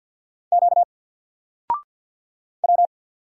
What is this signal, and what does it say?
Morse code, heard as a keyed tone, about 48 wpm, 700 Hz. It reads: Q W
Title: Single Character (Letter:Number:Punctuation:Prosign) Alt 50wpm